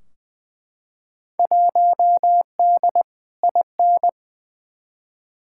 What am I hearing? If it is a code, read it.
Morse code, heard as a keyed tone, about 20 wpm, 700 Hz. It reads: 1D IN
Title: Field Day Exchanges - 20wpm